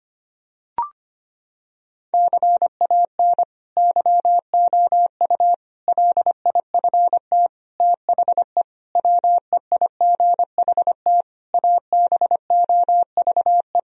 Morse code, heard as a keyed tone, about 25 wpm, 700 Hz. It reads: CAN YOU LIFT THE WEIGHT ABOVE
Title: Sentences from Top 1000 Words - Review Part 3 - 25wpm